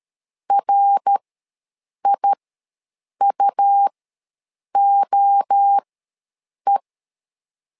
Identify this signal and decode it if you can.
Morse code, heard as a keyed tone, about 13 wpm, spaced as for 7 wpm, 800 Hz. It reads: RIUOE
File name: LEZIONI / 3 MWRO / 3 MWRO ws 12wpm 1914